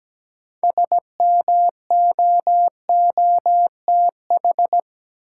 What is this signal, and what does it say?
Morse code, heard as a keyed tone, about 17 wpm, 700 Hz. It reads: SMOOTH
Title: Sentences from Top 1400 Words - New Word Drill - 17wpm